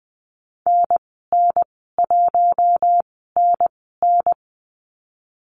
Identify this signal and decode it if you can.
Morse code, heard as a keyed tone, about 20 wpm, spaced as for 15 wpm, 700 Hz. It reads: NN1NN
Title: 042 Lesson 8g - Number 1 - Call Signs 20wpm